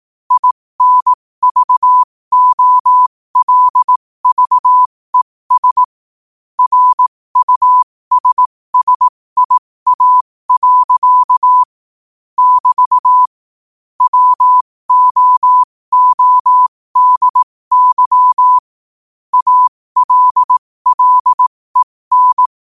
Morse code, heard as a keyed tone, about 18 wpm, spaced as for 15 wpm, 1000 Hz. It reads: INVOLVES RUSSIA. <BT> WOODY ALLEN
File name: qotdfiles / 15wpm